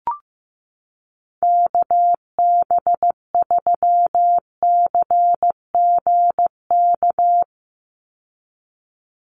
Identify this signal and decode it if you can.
Morse code, heard as a keyed tone, about 15 wpm, 700 Hz. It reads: KB3CGK
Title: USA Call Signs 15wpm